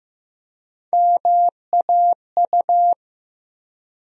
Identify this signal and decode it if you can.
Morse code, heard as a keyed tone, about 15 wpm, 700 Hz. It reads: MAU
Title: USA Call Sign Postfixes - 3 Letters 15wpm